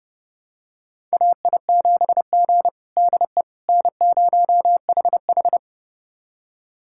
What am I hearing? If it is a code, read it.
Morse code, heard as a keyed tone, about 30 wpm, 700 Hz. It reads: AI7G DE N0HH
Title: USA Callsign DE Callsign - 4 Characters - 30wpm